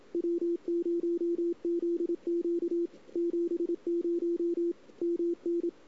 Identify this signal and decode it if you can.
Morse code, heard as a keyed tone, about 27 wpm, 340 Hz. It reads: W0ZQ 70 MN
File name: CW / 2008-ss-cw-k5tr-op-n5rz -02:00